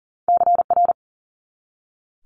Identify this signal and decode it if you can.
Morse code, heard as a keyed tone, about 42 wpm, 700 Hz. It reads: CR
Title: Multi-letter Phonemes 40wpm